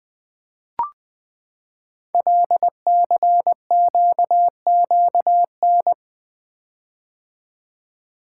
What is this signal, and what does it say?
Morse code, heard as a keyed tone, about 20 wpm, 700 Hz. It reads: LCQQN
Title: Code Groups - 5 Letters - 20wpm